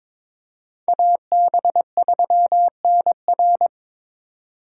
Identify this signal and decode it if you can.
Morse code, heard as a keyed tone, about 26 wpm, 700 Hz. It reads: AB3NR
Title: Speed Racing - USA Call Signs 22wpm